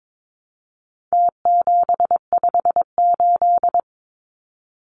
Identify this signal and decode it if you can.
Morse code, heard as a keyed tone, about 22 wpm, 700 Hz. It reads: T758
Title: Numbers - Contest Style - 4-Digit 22wpm